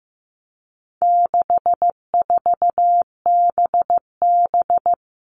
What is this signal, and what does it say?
Morse code, heard as a keyed tone, about 15 wpm, 700 Hz. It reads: 64BB
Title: All Cut Numbers - 4-Digit 15wpm